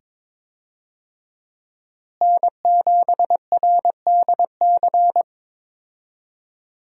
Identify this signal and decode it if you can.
Morse code, heard as a keyed tone, about 22 wpm, 700 Hz. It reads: N7RDC
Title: USA Call Signs - 5 Characters - Repeated 2x - 22wpm